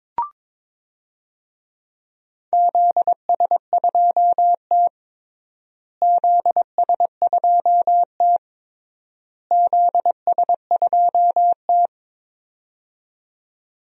Morse code, heard as a keyed tone, about 22 wpm, 700 Hz. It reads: ZS2T ZS2T ZS2T
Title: DXCC-100 Call Signs - 4 Characters - Repeated 3x - 22wpm